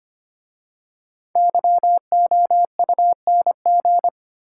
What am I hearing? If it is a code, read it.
Morse code, heard as a keyed tone, about 25 wpm, 700 Hz. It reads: YOUNG
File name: 5 Letter Words - First 3 Letters Given - 25wpm